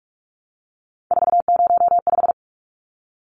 Morse code, heard as a keyed tone, about 44 wpm, 700 Hz. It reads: 405
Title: Numbers - 3-Digit 45wpm